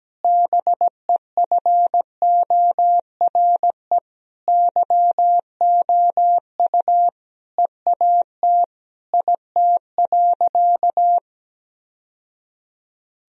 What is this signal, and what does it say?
Morse code, heard as a keyed tone, about 17 wpm, 700 Hz. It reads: BEFORE YOU EAT IT.
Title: Sentences from Top 1000 Words - Review Part 3 - 17wpm